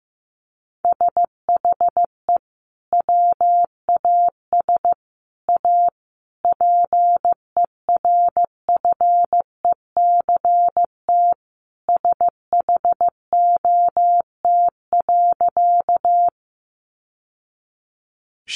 Morse code, heard as a keyed tone, about 15 wpm, 700 Hz. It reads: SHE WAS A PERFECT SHOT.
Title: Sentences from Top 1100 Words 15wpm